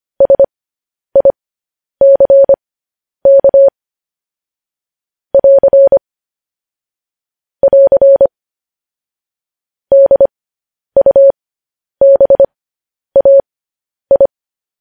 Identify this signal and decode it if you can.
Morse code, heard as a keyed tone, about 25 wpm, spaced as for 10 wpm, 550 Hz. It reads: SICK <AR> <AR> DUBAI
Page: news headlines in morse code at 10 wpm